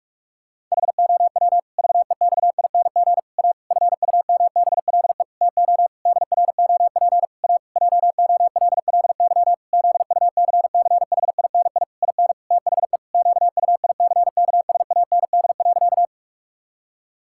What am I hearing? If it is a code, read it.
Morse code, heard as a keyed tone, about 43 wpm, 700 Hz. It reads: HOW VEXING A FUMBLE TO DROP A JOLLY ZUCCHINI IN THE QUICKSAND.
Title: Pangram Challenge v2 - 45wpm